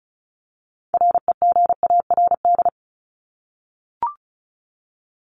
Morse code, heard as a keyed tone, about 34 wpm, 700 Hz. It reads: REGARD
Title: ING Suffix - Top 100 Words with Base Words 35wpm